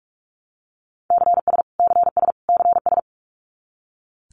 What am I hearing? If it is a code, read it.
Morse code, heard as a keyed tone, about 44 wpm, 700 Hz. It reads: CS CS CS
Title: Abbreviations Repeated 3x - 45wpm